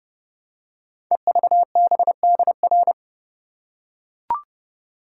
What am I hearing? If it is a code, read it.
Morse code, heard as a keyed tone, about 30 wpm, 700 Hz. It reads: EVBDR
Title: Code Groups - 5 Letter:Numbers - 30wpm